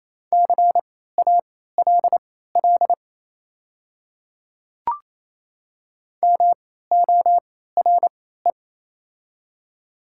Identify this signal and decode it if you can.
Morse code, heard as a keyed tone, about 28 wpm, spaced as for 16 wpm, 700 Hz. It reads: CALL MORE
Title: 3x Character Spacing - Top 200 Words - 28wpm